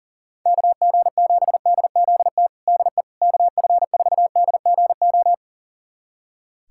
Morse code, heard as a keyed tone, about 37 wpm, 700 Hz. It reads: KG7DZT DE KF4DGO
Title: USA Callsign DE Callsign - 6 Characters - 40wpm